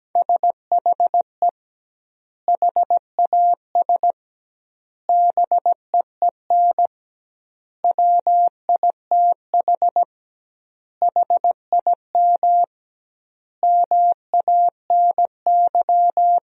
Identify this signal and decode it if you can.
Morse code, heard as a keyed tone, about 17 wpm, 700 Hz. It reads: SHE HAS BEEN WITH HIM MANY
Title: 2x Word-Spacing - Sentences from Top 200 Words - 17wpm